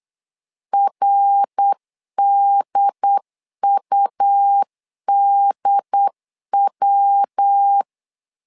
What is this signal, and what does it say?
Morse code, heard as a keyed tone, about 8 wpm, 800 Hz. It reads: RDUDW
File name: LEZIONI / 3 MWRO / 3 MWR 8wpm 2444